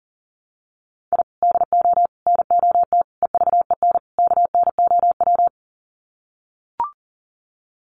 Morse code, heard as a keyed tone, about 40 wpm, 700 Hz. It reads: I DO NOT EVEN KNOW
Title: Sets of 5 Words 40wpm